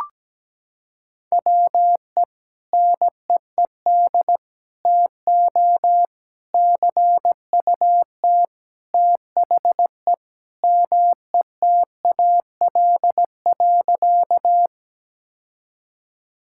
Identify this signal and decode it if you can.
Morse code, heard as a keyed tone, about 17 wpm, 700 Hz. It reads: WE NEED TO CUT THE METAL.
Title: Sentences from Top 1000 Words - Review Part 6 - 17wpm